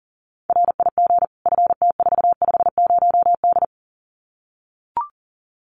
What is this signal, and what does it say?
Morse code, heard as a keyed tone, about 40 wpm, 700 Hz. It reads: RIG FT450D
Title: Common QSO Phrases 40wpm